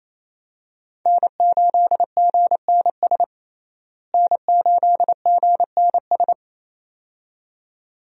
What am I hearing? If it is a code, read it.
Morse code, heard as a keyed tone, about 28 wpm, 700 Hz. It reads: N8GNS N8GNS
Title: USA Call Signs - 5 Characters - Repeated 2x - 28wpm